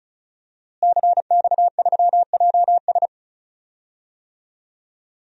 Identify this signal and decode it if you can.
Morse code, heard as a keyed tone, about 34 wpm, 700 Hz. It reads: CX3JS
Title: DXCC-100 Call Signs - 5 Characters - Repeated 2x - 35wpm